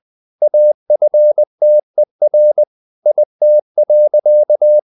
Morse code, heard as a keyed tone, about 20 wpm, 600 Hz. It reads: AFTER IT.
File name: the old man and the sea part 2 20wpm